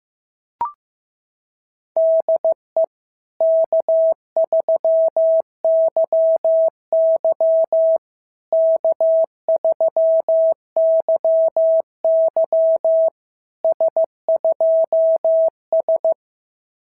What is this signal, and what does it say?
Morse code, heard as a keyed tone, about 15 wpm, 650 Hz. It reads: DE K3YY K3YY S2S
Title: SOTA QSO Practice 15wpm